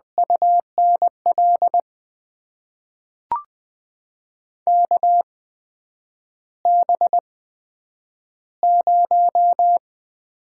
Morse code, heard as a keyed tone, about 20 wpm, spaced as for 9 wpm, 700 Hz. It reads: UNL KB0
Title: USA Call Signs - 6 Characters - 8x Character Spacing - 20wpm